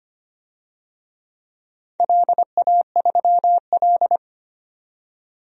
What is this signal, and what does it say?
Morse code, heard as a keyed tone, about 25 wpm, 700 Hz. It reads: LA3L
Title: DXCC-100 Call Signs - 4 Characters - 25wpm